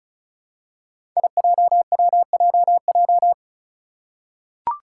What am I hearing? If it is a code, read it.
Morse code, heard as a keyed tone, about 34 wpm, 700 Hz. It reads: IJWJJ
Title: Five Challenging Letter-Number Combinations 35wpm